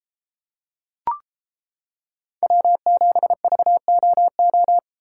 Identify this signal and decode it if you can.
Morse code, heard as a keyed tone, about 30 wpm, 700 Hz. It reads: W7VOO